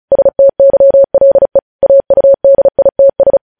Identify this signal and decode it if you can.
Morse code, heard as a keyed tone, about 36 wpm, 550 Hz. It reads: STYLE AUDITS
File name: podcast / News headlines-1x-35x35